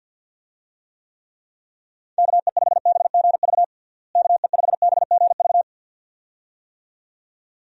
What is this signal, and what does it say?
Morse code, heard as a keyed tone, about 52 wpm, 700 Hz. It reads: KE5BGV KE5BGV
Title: USA Call Signs - 6 Characters - Repeated 2x - 50wpm